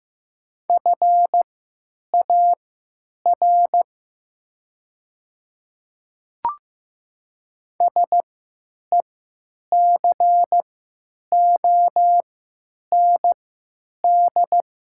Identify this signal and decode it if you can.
Morse code, heard as a keyed tone, about 15 wpm, spaced as for 8 wpm, 700 Hz. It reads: FAR SECOND
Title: 3x Character Spacing - Top 300 Words - 15wpm